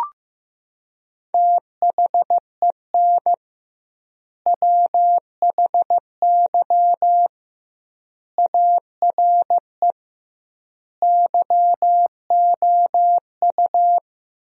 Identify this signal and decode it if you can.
Morse code, heard as a keyed tone, about 15 wpm, 700 Hz. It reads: THEN WHY ARE YOU
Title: 2x Word-Spacing - Sentences from Top 200 Words - 15wpm